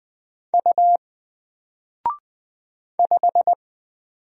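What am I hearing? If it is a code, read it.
Morse code, heard as a keyed tone, about 20 wpm, 700 Hz. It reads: U 5
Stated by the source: Single letter and number Morse drill at 20 wpm, from N4LCD.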